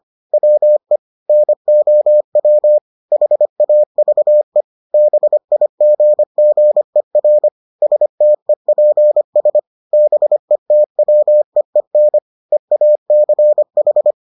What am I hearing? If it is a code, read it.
Morse code, heard as a keyed tone, about 25 wpm, 600 Hz. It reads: WE NOW HAVE BIGGER STEPS BETWEEN EACH